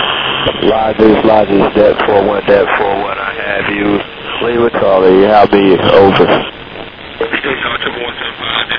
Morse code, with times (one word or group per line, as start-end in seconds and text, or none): none